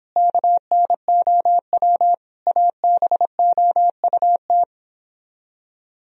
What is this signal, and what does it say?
Morse code, heard as a keyed tone, about 31 wpm, 700 Hz. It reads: KNOW ABOUT